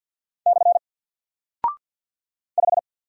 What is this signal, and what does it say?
Morse code, heard as a keyed tone, about 48 wpm, 700 Hz. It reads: / 5
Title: Single Character (Letter:Number:Punctuation:Prosign) Alt 50wpm